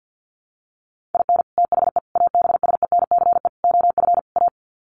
Morse code, heard as a keyed tone, about 50 wpm, 700 Hz. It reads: IN THE ABSENCE OF A
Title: Sets of 5 Words Encore 50wpm